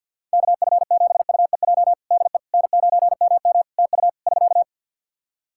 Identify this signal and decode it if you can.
Morse code, heard as a keyed tone, about 50 wpm, 700 Hz. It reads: KF7UEJ DE N9MM TU <SK>
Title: Practice QSO - Exchange 4 - Ending - 50wpm